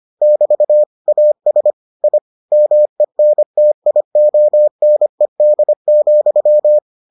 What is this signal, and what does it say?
Morse code, heard as a keyed tone, about 25 wpm, 600 Hz. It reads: <BT> AS I MENTIONED,